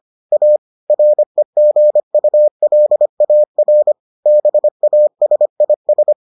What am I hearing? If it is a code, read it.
Morse code, heard as a keyed tone, about 25 wpm, 600 Hz. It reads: A REGULAR BASIS